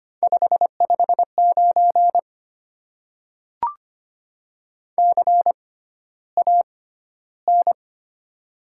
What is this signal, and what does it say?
Morse code, heard as a keyed tone, about 25 wpm, spaced as for 13 wpm, 700 Hz. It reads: 559 CAN
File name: Top 90 QSO Elements - 6x Character Spacing - 25wpm